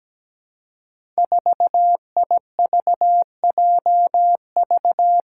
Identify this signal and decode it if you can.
Morse code, heard as a keyed tone, about 17 wpm, 700 Hz. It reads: 4IVJV
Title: Five Challenging Letter-Number Combinations 17wpm